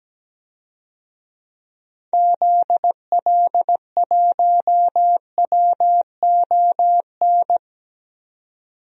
Morse code, heard as a keyed tone, about 17 wpm, 700 Hz. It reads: ZL1WON
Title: DXCC-100 Call Signs - 6 Characters - 17wpm